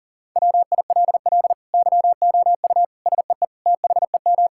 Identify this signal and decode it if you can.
Morse code, heard as a keyed tone, about 40 wpm, 700 Hz. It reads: WILL YOU SEE THEM
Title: Sentences from Top 1600 Words 40wpm